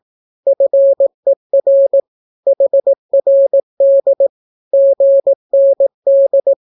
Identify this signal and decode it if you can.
Morse code, heard as a keyed tone, about 18 wpm, 550 Hz. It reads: FER HRD GND
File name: naukapodstaw / skroty / skroty